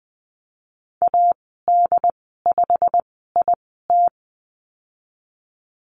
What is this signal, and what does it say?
Morse code, heard as a keyed tone, about 20 wpm, spaced as for 15 wpm, 700 Hz. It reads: AD5IT
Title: 147 Lesson 24g - Letter y - Call Signs 20wpm